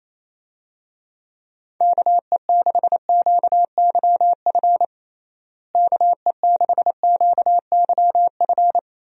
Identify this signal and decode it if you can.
Morse code, heard as a keyed tone, about 27 wpm, 700 Hz. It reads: KE6QYF KE6QYF
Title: USA Call Signs - 6 Characters - Repeated 3x - 28wpm